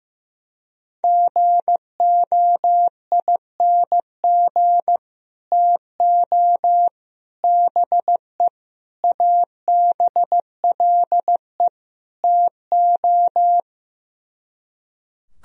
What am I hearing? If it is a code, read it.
Morse code, heard as a keyed tone, about 16 wpm, 700 Hz. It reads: GOING TO BE ABLE TO
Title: Speed-Racing - Sets of 5 Words 15wpm